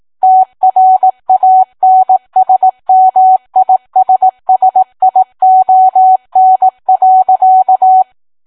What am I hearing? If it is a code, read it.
Morse code, heard as a keyed tone, about 18 wpm, 750 Hz. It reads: TRANSMISSION.